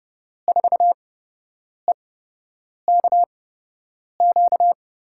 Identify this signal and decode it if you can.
Morse code, heard as a keyed tone, about 30 wpm, spaced as for 8 wpm, 700 Hz. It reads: 4EKQ
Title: USA Call Signs - 6 Characters - 8x Character Spacing - 30wpm